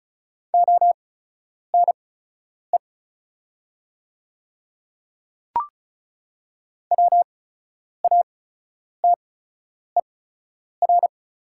Morse code, heard as a keyed tone, about 34 wpm, spaced as for 10 wpm, 700 Hz. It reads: ONE WATER
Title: 8x Character Spacing - Top 100 Words - 35wpm